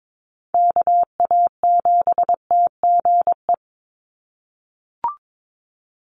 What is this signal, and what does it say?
Morse code, heard as a keyed tone, about 22 wpm, 700 Hz. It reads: KA7TGE
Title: USA Call Signs 22wpm